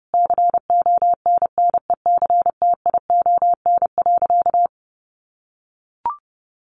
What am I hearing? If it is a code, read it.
Morse code, heard as a keyed tone, about 30 wpm, 700 Hz. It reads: CONNECTION.
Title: Sentences from Top 1300 Words 30wpm